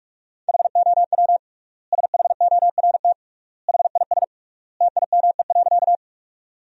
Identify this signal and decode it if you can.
Morse code, heard as a keyed tone, about 42 wpm, 700 Hz. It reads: HOW SHORT HIS TIME.
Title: 3x Word-Spacing - Sentences from Top 400 Words - 45wpm